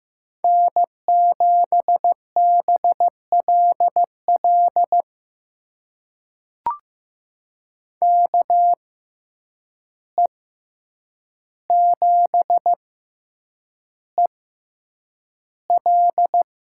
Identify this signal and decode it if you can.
Morse code, heard as a keyed tone, about 15 wpm, spaced as for 8 wpm, 700 Hz. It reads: N7BLL KE7EL
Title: USA Call Signs - 5 Characters - 6x Character Spacing - 15wpm